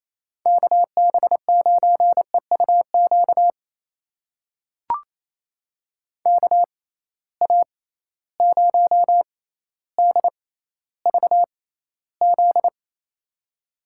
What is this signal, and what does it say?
Morse code, heard as a keyed tone, about 28 wpm, spaced as for 14 wpm, 700 Hz. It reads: KB9EUQ KA0DVZ
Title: USA Call Signs - 6 Characters - 6x Character Spacing - 28wpm